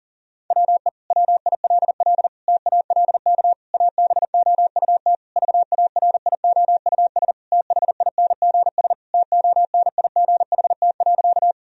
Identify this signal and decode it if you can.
Morse code, heard as a keyed tone, about 40 wpm, 700 Hz. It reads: WE WILL TALK ABOUT VARIOUS THINGS TONIGHT.
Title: Sentences from Top 1000 Words - Review Part 4 - 40wpm